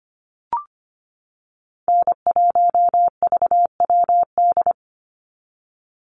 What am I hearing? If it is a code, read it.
Morse code, heard as a keyed tone, about 25 wpm, 700 Hz. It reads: N1VWD
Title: NA SOTA Chaser Callsigns 25wpm